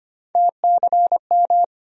Morse code, heard as a keyed tone, about 25 wpm, 700 Hz. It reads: TCM
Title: USA Call Sign Postfixes - 3 Letters 25wpm